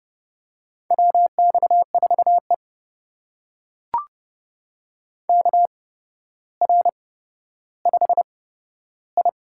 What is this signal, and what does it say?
Morse code, heard as a keyed tone, about 30 wpm, spaced as for 13 wpm, 700 Hz. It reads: WX4E KR5I
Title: USA Call Signs - 4 Characters - 8x Character Spacing - 30wpm